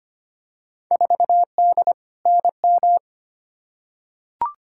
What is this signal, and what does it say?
Morse code, heard as a keyed tone, about 25 wpm, 700 Hz. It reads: 4D NM
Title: Field Day Exchanges Encore - 25wpm